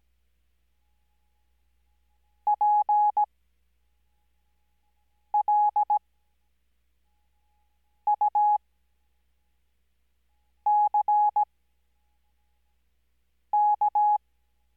Morse code, heard as a keyed tone, about 17 wpm, spaced as for 4 wpm, 840 Hz. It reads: PLUCK
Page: Farnsworth Spacing: 15WPM Character speed Spaced at 5WPM.